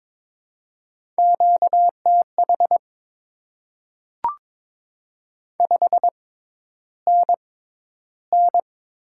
Morse code, heard as a keyed tone, about 22 wpm, spaced as for 11 wpm, 700 Hz. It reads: QTH 5NN
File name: Top 20 QSO Elements - 6x Character Spacing - 22wpm